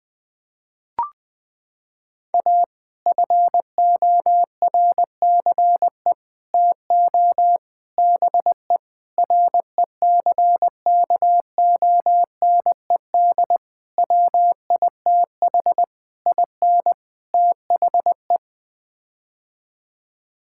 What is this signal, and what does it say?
Morse code, heard as a keyed tone, about 20 wpm, 700 Hz. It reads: A FORCE TO BE RECKONED WITH IN THE
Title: Sets of 8 Words - 20wpm